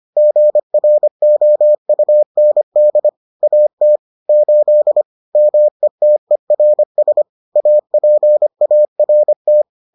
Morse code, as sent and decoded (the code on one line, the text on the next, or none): --. .-. --- ..- -. -.. / .- - / ---.. / -- . - . .-. ... / .- .--. .- .-. -
GROUND AT 8 METERS APART